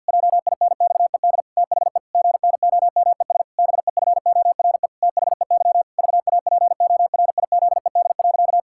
Morse code, heard as a keyed tone, about 50 wpm, 700 Hz. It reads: JINXED THE GNOMES BEFORE THEY VAPORIZED.
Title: Pangram Challenge v2 - 50wpm